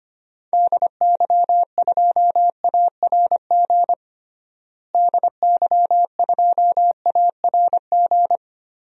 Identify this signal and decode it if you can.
Morse code, heard as a keyed tone, about 25 wpm, 700 Hz. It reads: DY2ARG DY2ARG
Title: DXCC-100 Call Signs - 6 Characters - Repeated 3x - 25wpm